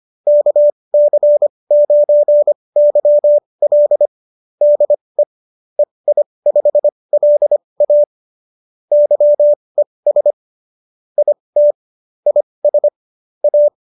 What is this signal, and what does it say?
Morse code, heard as a keyed tone, about 25 wpm, spaced as for 20 wpm, 590 Hz. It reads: KC9YL DE EI5LA YES IT IS A